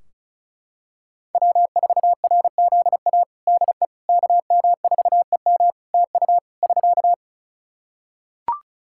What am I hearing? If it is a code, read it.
Morse code, heard as a keyed tone, about 34 wpm, 700 Hz. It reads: W4RZA DE KM4EM TU <SK>
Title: Practice QSO - Exchange 4 - Ending - 35wpm